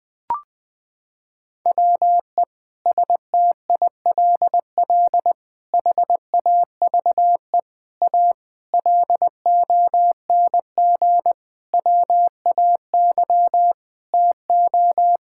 Morse code, heard as a keyed tone, about 20 wpm, 700 Hz. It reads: WE STILL HAVE A LONG WAY TO
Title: Sentences with 5-Word Phrases - 20wpm